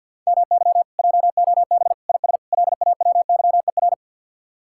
Morse code, heard as a keyed tone, about 64 wpm, 700 Hz. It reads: MY JOB IS LAWYER